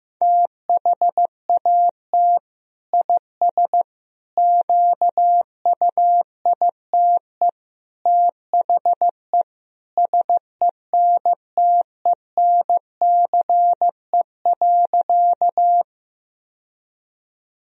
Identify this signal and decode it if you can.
Morse code, heard as a keyed tone, about 15 wpm, 700 Hz. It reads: THAT IS QUITE THE SENTENCE.
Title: Sentences from Top 1000 Words - Review Part 1 - 15wpm